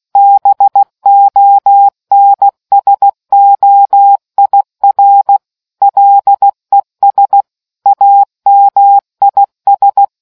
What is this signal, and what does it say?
Morse code, heard as a keyed tone, about 16 wpm, 800 Hz. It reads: BONSOIR LES AMIS